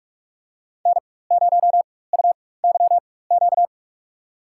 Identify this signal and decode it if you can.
Morse code, heard as a keyed tone, about 46 wpm, spaced as for 21 wpm, 700 Hz. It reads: N0UYQ
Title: USA Call Signs - 5 Characters - 4x Character Spacing - 45wpm